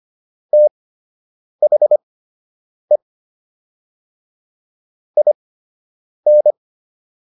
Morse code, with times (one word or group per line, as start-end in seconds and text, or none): 0.5-3.0 THE
5.2-6.5 IN